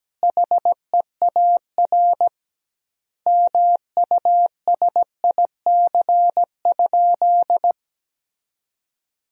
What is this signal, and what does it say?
Morse code, heard as a keyed tone, about 17 wpm, 700 Hz. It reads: HEAR MUSIC?
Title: 2x Word-Spacing - Sentences from Top 300 Words - 17wpm